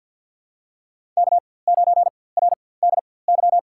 Morse code, heard as a keyed tone, about 51 wpm, spaced as for 23 wpm, 700 Hz. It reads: K9RDY
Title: USA Call Signs - 5 Characters - 4x Character Spacing - 50wpm